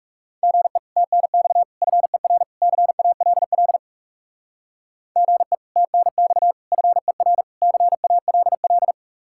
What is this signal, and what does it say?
Morse code, heard as a keyed tone, about 37 wpm, 700 Hz. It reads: GE TNX FER CALL GE TNX FER CALL
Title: Speed-Racing Common QSO Phrases - 30wpm